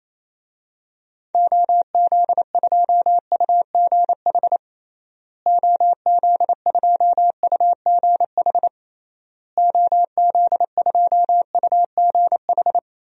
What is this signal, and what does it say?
Morse code, heard as a keyed tone, about 28 wpm, 700 Hz. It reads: OZ2UGH OZ2UGH OZ2UGH
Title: DXCC-100 Call Signs - 6 Characters - Repeated 3x - 28wpm